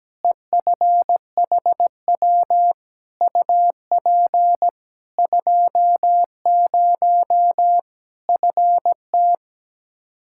Morse code, heard as a keyed tone, about 17 wpm, 700 Hz. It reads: EFHW UP 20 FT
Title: K1USN SST QSO Practice 17wpm